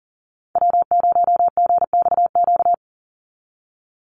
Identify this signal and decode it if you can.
Morse code, heard as a keyed tone, about 40 wpm, 700 Hz. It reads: W0GXQ
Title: Call Signs Repeated 3x - 40wpm